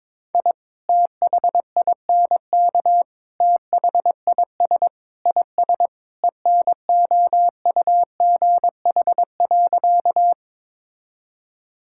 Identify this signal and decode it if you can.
Morse code, heard as a keyed tone, about 22 wpm, 700 Hz. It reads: I THINK THIS IS ENOUGH.
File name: practice / 22wpm / Sentences with 4-Word Phrases - 22wpm